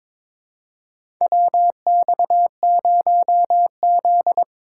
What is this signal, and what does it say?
Morse code, heard as a keyed tone, about 22 wpm, 700 Hz. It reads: WX0Z
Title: K1USN SST QSO Practice - 22wpm